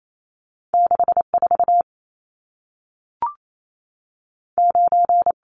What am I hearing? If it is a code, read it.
Morse code, heard as a keyed tone, about 28 wpm, 700 Hz. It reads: 6 4 9
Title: Numbers - 2-Digit 28wpm